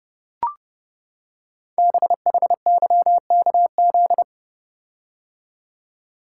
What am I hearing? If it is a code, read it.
Morse code, heard as a keyed tone, about 30 wpm, 700 Hz. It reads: BHYKZ
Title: Code Groups - 5 Letters - 30wpm